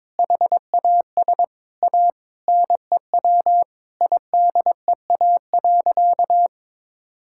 Morse code, heard as a keyed tone, about 22 wpm, 700 Hz. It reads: HAS A NEW IDEA.
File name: practice / 22wpm / Sentences with 2-Word Phrases - 22wpm